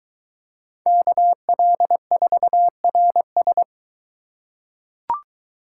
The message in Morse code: KL4RS

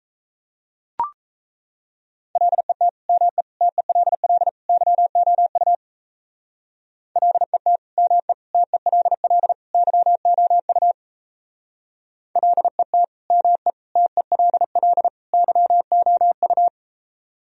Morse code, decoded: LET ME TELL YOU LET ME TELL YOU LET ME TELL YOU

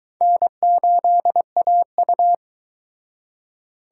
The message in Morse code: N8AU

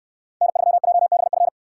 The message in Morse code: N3JBF

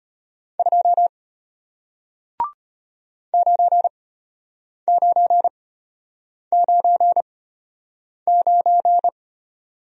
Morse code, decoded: 2 9999

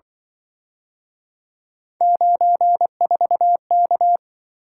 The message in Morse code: 94K

